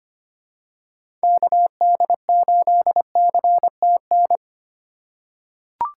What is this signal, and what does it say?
Morse code, heard as a keyed tone, about 25 wpm, 700 Hz. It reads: KD8CTN
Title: USA Call Signs - 6 Characters - 4x Character Spacing - 25wpm